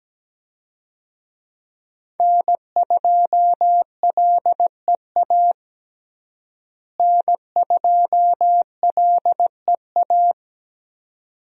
Morse code, heard as a keyed tone, about 17 wpm, 700 Hz. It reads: N2LEA N2LEA
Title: USA Call Signs - 5 Characters - Repeated 2x - 17wpm